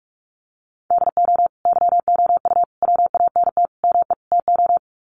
Morse code, heard as a keyed tone, about 44 wpm, 700 Hz. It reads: DO YOU WANT ME TO
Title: Sets of 5 Words 45wpm